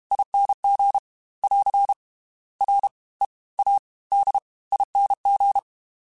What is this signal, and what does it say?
Morse code, heard as a keyed tone, about 32 wpm, 800 Hz. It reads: ING <AR> R E A D ING
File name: Special INGs / Special INGs-mp3s / Fast ing @30